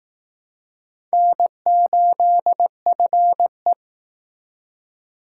Morse code, heard as a keyed tone, about 19 wpm, 700 Hz. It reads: N8FE